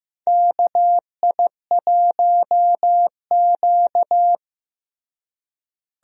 Morse code, heard as a keyed tone, about 15 wpm, 700 Hz. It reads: KI1Q
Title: USA Call Signs - 4 Characters - Repeated 2x - 15wpm